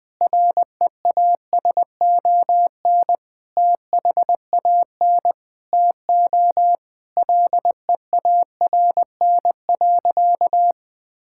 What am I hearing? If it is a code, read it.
Morse code, heard as a keyed tone, about 20 wpm, 700 Hz. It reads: REASON THAN TO LEARN.
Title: Sentences with 7-Word Phrases - Spoken Beginning - 20wpm